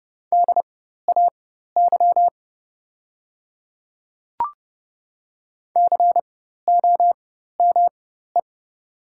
Morse code, heard as a keyed tone, about 30 wpm, spaced as for 14 wpm, 700 Hz. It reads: DAY COME
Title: 4x Character Spacing - Top 100 Words - 30wpm